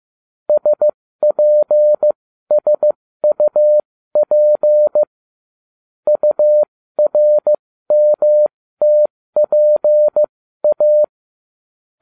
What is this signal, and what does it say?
Morse code, heard as a keyed tone, about 15 wpm, spaced as for 13 wpm, 600 Hz. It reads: SPSUP URMTPA